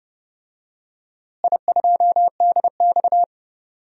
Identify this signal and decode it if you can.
Morse code, heard as a keyed tone, about 31 wpm, 700 Hz. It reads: I2DX
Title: DXCC-100 Call Signs - 4 Characters - Repeated 2x - 30wpm